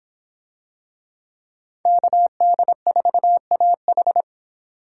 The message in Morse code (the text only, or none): KD4AH